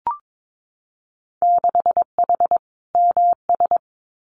Morse code, heard as a keyed tone, about 22 wpm, 700 Hz. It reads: 6H MS
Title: Winter Field Day Exchanges Encore 22wpm